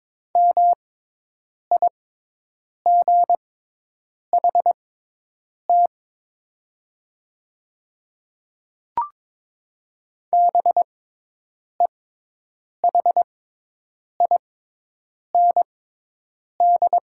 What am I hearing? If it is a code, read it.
Morse code, heard as a keyed tone, about 22 wpm, spaced as for 8 wpm, 700 Hz. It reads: MIGHT BEHIND
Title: 6x Character Spacing - Top 500 Words - 22wpm